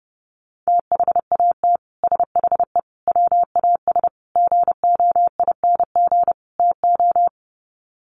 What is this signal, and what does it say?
Morse code, heard as a keyed tone, about 30 wpm, 700 Hz. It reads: THAT SHE WAS GOING TO
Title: Sets of 5 Words Encore2 30wpm